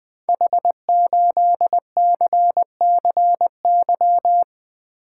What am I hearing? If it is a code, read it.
Morse code, heard as a keyed tone, about 20 wpm, 700 Hz. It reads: H8CCY
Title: Code Groups - 5 Letter:Numbers - 20wpm